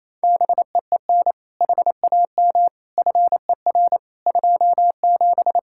Morse code, heard as a keyed tone, about 28 wpm, 700 Hz. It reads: BEEN HAM FER 27
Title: K1USN SST QSO Practice 28wpm